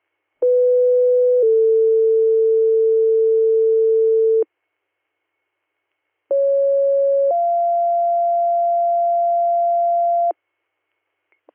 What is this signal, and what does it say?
Two-tone page, heard as a one-second tone, then a three-second tone, about 500 then 450 Hz, about 550 then 690 Hz. It is heard from 0.4 to 10.3 s.